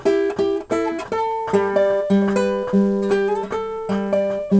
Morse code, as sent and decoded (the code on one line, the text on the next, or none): none